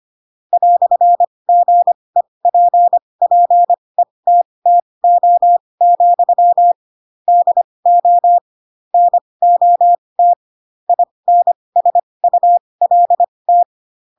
Morse code, heard as a keyed tone, about 25 wpm, spaced as for 20 wpm, 700 Hz. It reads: "GEPPETTO, DO NOT INSULT
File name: Pinocchio / Pinocchio ch2 15 25 ik1hge